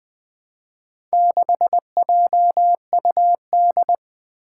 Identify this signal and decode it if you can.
Morse code, heard as a keyed tone, about 20 wpm, 700 Hz. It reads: 6JUD
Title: Four Challenging Letter-Number Combinations 20wpm